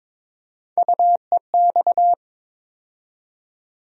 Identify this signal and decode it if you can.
Morse code, heard as a keyed tone, about 22 wpm, 700 Hz. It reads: UEX
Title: USA Call Sign Postfixes - 3 Letters 22wpm